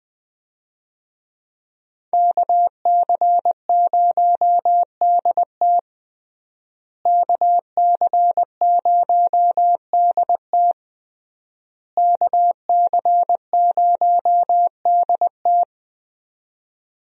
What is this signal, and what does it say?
Morse code, heard as a keyed tone, about 20 wpm, 700 Hz. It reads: KC0DT KC0DT KC0DT
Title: USA Call Signs - 5 Characters - Repeated 3x - 20wpm